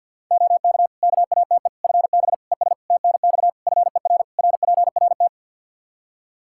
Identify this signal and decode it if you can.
Morse code, heard as a keyed tone, about 52 wpm, 700 Hz. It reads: OK KATE FB ES TNX FER RPRT
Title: Practice QSO - Exchange 2 - Info - 50wpm